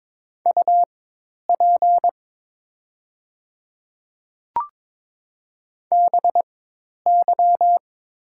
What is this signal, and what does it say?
Morse code, heard as a keyed tone, about 21 wpm, spaced as for 10 wpm, 700 Hz. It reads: UP BY
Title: 4x Character Spacing - Top 100 Words - 22wpm